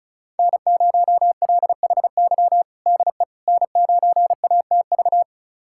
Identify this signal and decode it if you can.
Morse code, heard as a keyed tone, about 36 wpm, 700 Hz. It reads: N0LHY DE N9ATV
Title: USA Callsign DE Callsign - 5 Characters - 35wpm